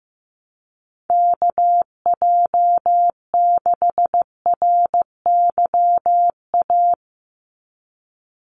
Morse code, heard as a keyed tone, about 15 wpm, 700 Hz. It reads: KJ6RYA
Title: USA Call Signs 15wpm